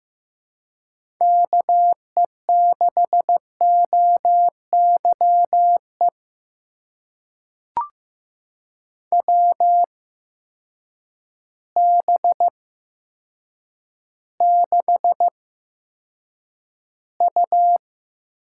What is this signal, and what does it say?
Morse code, heard as a keyed tone, about 15 wpm, spaced as for 8 wpm, 700 Hz. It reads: KE6OYE WB6U
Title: USA Call Signs - 6 Characters - 8x Character Spacing - 15wpm